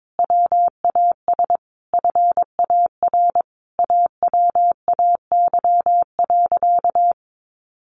Morse code, heard as a keyed tone, about 22 wpm, 700 Hz. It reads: WAS FAR AWAY.
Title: Sentences from Top 1500 Words 22wpm